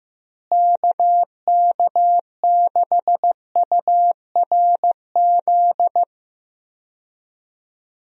Morse code, heard as a keyed tone, about 15 wpm, 700 Hz. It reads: KK6URZ
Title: USA Call Signs - 6 Characters - Repeated 3x - 15wpm